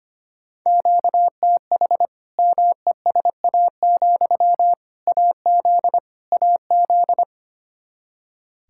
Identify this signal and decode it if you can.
Morse code, heard as a keyed tone, about 25 wpm, 700 Hz. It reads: QTH MESA, AZ AZ
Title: Practice QSO - Exchange 1 - Beginning - 25wpm